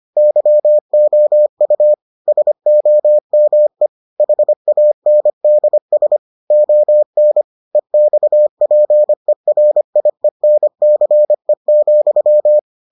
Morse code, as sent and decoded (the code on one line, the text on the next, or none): -.-- --- ..- / ... --- -- . / .... .- -. -.. ... / --- -. / . -..- .--. .-. .. -. -.-. --..--
YOU SOME HANDS ON EXPERIENCE,